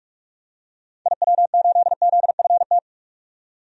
Morse code, heard as a keyed tone, about 46 wpm, 700 Hz. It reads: IW8ZFT
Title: DXCC-100 Call Signs - 6 Characters - 45wpm